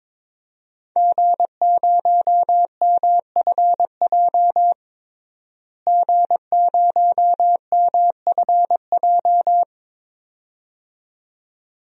Morse code, heard as a keyed tone, about 22 wpm, 700 Hz. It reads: G0MFJ G0MFJ
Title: DXCC-100 Call Signs - 5 Characters - Repeated 2x - 22wpm